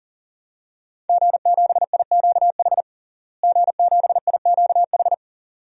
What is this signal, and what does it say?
Morse code, heard as a keyed tone, about 37 wpm, 700 Hz. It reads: G7IQH G7IQH
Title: DXCC-100 Call Signs - 5 Characters - Repeated 2x - 40wpm